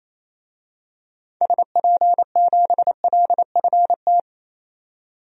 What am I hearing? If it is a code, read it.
Morse code, heard as a keyed tone, about 27 wpm, 700 Hz. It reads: SP7LFT
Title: DXCC-100 Call Signs - 6 Characters - 28wpm